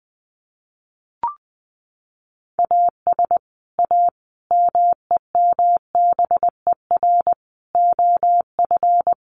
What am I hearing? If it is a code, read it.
Morse code, heard as a keyed tone, about 20 wpm, 700 Hz. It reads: AS A MEMBER OF
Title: Sets of 5 Words 20wpm